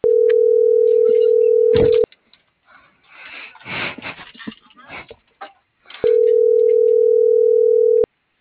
Ringback tone (North America), 440 + 480 Hz.